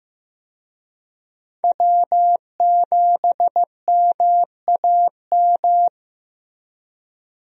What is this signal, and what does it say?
Morse code, heard as a keyed tone, about 15 wpm, 700 Hz. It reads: W7MAM
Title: USA Call Signs - 5 Characters - Repeated 3x - 15wpm